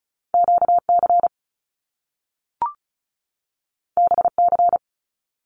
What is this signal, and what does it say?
Morse code, heard as a keyed tone, about 36 wpm, 700 Hz. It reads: QC BC
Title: Canadian Provinces and Territories 35wpm